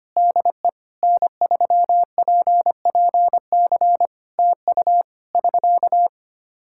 Morse code, heard as a keyed tone, about 25 wpm, 700 Hz. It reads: DE N3PPC TU <SK>
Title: Practice QSO - Exchange 4 - Ending - 25wpm